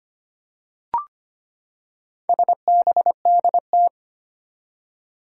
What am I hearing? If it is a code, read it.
Morse code, heard as a keyed tone, about 25 wpm, 700 Hz. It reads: SBDT
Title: Four Challenging Letter-Number Combinations 25wpm